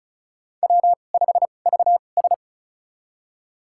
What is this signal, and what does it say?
Morse code, heard as a keyed tone, about 34 wpm, spaced as for 25 wpm, 700 Hz. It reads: W5VS
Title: USA Call Signs - 4 Characters - 2x Character Spacing - 35wpm